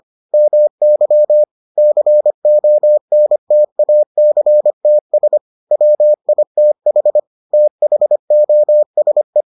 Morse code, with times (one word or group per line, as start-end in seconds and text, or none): 0.3-1.4 MY
1.8-5.4 CONTACTS
5.7-7.2 WITH
7.5-9.4 THOSE